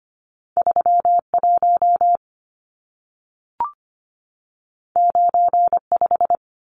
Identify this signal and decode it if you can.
Morse code, heard as a keyed tone, about 25 wpm, 700 Hz. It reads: 31 95
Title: Numbers - 2-Digit 25wpm